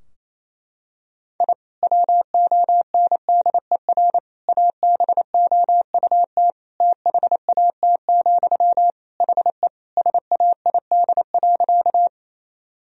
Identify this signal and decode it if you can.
Morse code, heard as a keyed tone, about 28 wpm, 700 Hz. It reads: I WONDER ABOUT THAT, HE SAID.
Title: Sentences from Top 1000 Words - Review Part 3 - 28wpm